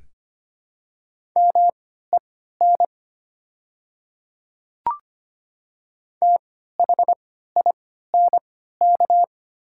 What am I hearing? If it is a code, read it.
Morse code, heard as a keyed tone, about 25 wpm, spaced as for 14 wpm, 690 Hz. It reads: MEN THINK
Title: 3x Character Spacing - Top 200 Words - 25wpm